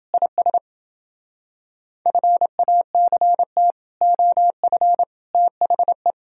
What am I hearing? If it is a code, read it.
Morse code, heard as a keyed tone, about 30 wpm, 700 Hz. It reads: IS FACT OF THE